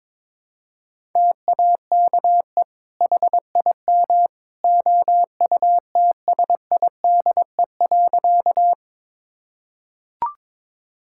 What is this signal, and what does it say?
Morse code, heard as a keyed tone, about 22 wpm, 700 Hz. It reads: TAKE HIM OUTSIDE.
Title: Sentences from Top 1000 Words - Review Part 6 - 22wpm